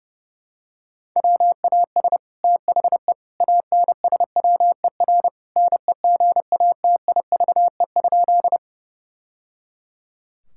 Morse code, heard as a keyed tone, about 30 wpm, 700 Hz. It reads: WAS THE ANSWER NEGATIVE?
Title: Sentences from Top 1000 Words - Review Part 6 - 30wpm